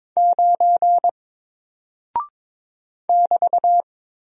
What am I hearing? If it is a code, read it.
Morse code, heard as a keyed tone, about 22 wpm, 700 Hz. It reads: 9 <BT>
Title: Single Character (Letter:Number:Punctuation:Prosign) 22wpm